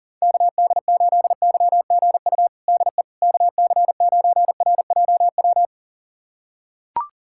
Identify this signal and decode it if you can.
Morse code, heard as a keyed tone, about 37 wpm, 700 Hz. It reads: KD8YGU DE KC9RJW
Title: USA Callsign DE Callsign - 6 Characters - 40wpm